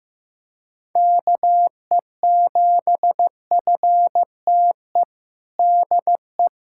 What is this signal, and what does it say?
Morse code, heard as a keyed tone, about 15 wpm, 700 Hz. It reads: KE7FTE DE
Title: USA Callsign DE Callsign - 6 Characters - 15wpm